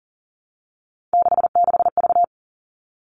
Morse code, heard as a keyed tone, about 40 wpm, 700 Hz. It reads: 66V
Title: All Cut Numbers - 3-Digit 40wpm